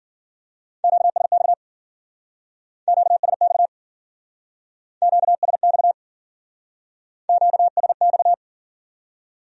Morse code, heard as a keyed tone, about 52 wpm, 700 Hz. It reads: QSX QSX QSX QSX